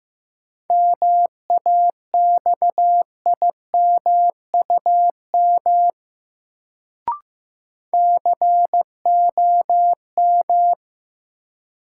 Morse code, heard as a keyed tone, about 15 wpm, 700 Hz. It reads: MAXIMUM COM
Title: Sentences from Top 1300 Words - New Word Drill - 15wpm